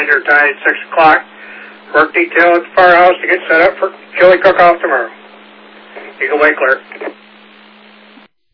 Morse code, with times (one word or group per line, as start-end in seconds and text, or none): none